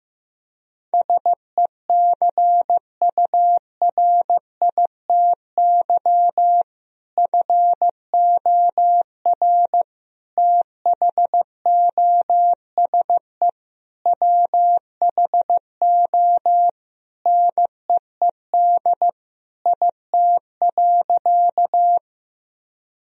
Morse code, heard as a keyed tone, about 15 wpm, 700 Hz. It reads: SECURITY FOR THOSE WHO NEED IT.
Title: Sentences from Top 1000 Words - Review Part 4 - 15wpm